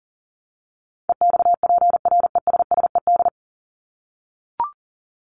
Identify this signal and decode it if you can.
Morse code, heard as a keyed tone, about 40 wpm, 700 Hz. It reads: EXPRESSED
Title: ED Suffix - Top 100 Words with Base Words 40wpm